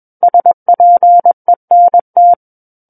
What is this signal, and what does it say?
Morse code, heard as a keyed tone, about 21 wpm, 700 Hz. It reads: SPENT
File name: Book One / 12-Chapter 12 - 21 WPM